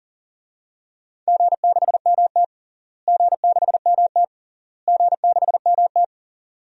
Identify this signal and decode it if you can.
Morse code, heard as a keyed tone, about 40 wpm, 700 Hz. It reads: G6MT G6MT G6MT
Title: DXCC-100 Call Signs - 4 Characters - Repeated 3x - 40wpm